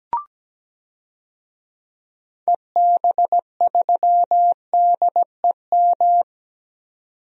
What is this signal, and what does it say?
Morse code, heard as a keyed tone, about 17 wpm, 700 Hz. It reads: EB3DEM
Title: DXCC-100 Call Signs - 6 Characters - 17wpm